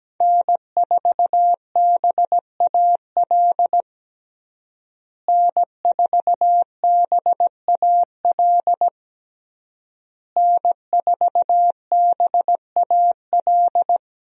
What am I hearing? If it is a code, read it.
Morse code, heard as a keyed tone, about 17 wpm, 700 Hz. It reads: N4BAL N4BAL N4BAL
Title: USA Call Signs - 5 Characters - Repeated 3x - 17wpm